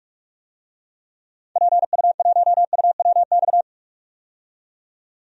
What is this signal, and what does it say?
Morse code, heard as a keyed tone, about 41 wpm, 700 Hz. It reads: PU1UWX
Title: DXCC-100 Call Signs - 6 Characters - 45wpm